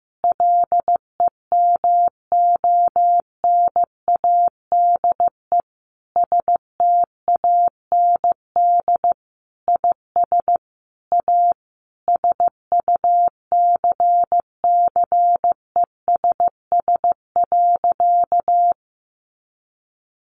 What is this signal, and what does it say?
Morse code, heard as a keyed tone, about 15 wpm, 700 Hz. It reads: LEMONADE STAND IS A SUCCESS.
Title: Additional Easy Phrases 15wpm